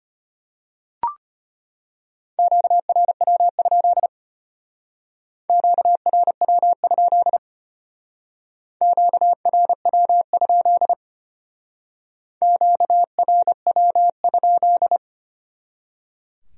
Morse code, heard as a keyed tone, about 30 wpm, 700 Hz. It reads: QRW? QRW? QRW? QRW?
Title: Speed-Racing Q-Codes - 25wpm